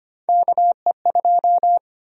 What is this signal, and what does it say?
Morse code, heard as a keyed tone, about 25 wpm, 700 Hz. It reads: KE2
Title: USA Call Sign Prefixes - Repeated 2x - 25wpm